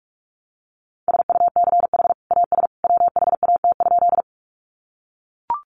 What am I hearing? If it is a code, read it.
Morse code, heard as a keyed tone, about 44 wpm, 700 Hz. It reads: SUCH AS WHAT?
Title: Sentences from Top 200 Words 45wpm